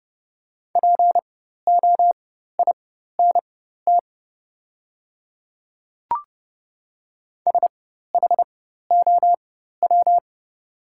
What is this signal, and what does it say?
Morse code, heard as a keyed tone, about 30 wpm, spaced as for 14 wpm, 700 Hz. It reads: POINT SHOW
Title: 4x Character Spacing - Top 300 Words - 30wpm